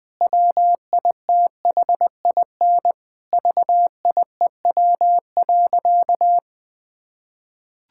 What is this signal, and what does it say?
Morse code, heard as a keyed tone, about 21 wpm, 700 Hz. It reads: WITHIN VIEW.
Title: Sentences from Top 1000 Words - Review Part 6 - 20wpm